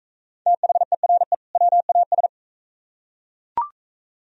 Morse code, heard as a keyed tone, about 44 wpm, 700 Hz. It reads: THERE WAS